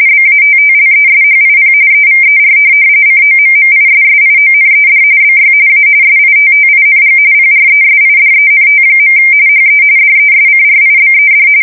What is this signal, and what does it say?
RTTY, about 45.45 baud. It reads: BATTLESHIP EVER COMMISSIONED BY A SITTING PRESIDENT. HE DESCRIBED
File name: fsk170 / USS-New-Jersey